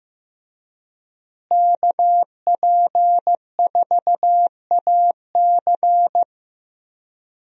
Morse code, heard as a keyed tone, about 15 wpm, 700 Hz. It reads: KP4AC
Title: USA Call Signs - 5 Characters - Repeated 3x - 15wpm